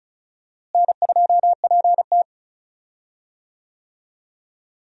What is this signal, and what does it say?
Morse code, heard as a keyed tone, about 36 wpm, 700 Hz. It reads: N2PT
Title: USA Call Signs - 4 Characters - Repeated 2x - 35wpm